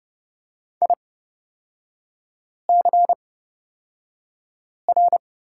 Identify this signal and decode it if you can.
Morse code, heard as a keyed tone, about 30 wpm, spaced as for 5 wpm, 700 Hz. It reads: ICR
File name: Lesson 37e - Letter x - Three Characters 30wpm